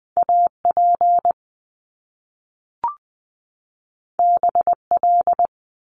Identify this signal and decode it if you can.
Morse code, heard as a keyed tone, about 20 wpm, 700 Hz. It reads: AP BL
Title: Top Two Letter Combinations in Words 20wpm